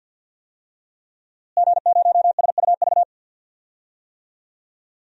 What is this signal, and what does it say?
Morse code, heard as a keyed tone, about 49 wpm, 700 Hz. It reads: G0SUV
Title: DXCC-100 Call Signs - 5 Characters - Repeated 2x - 50wpm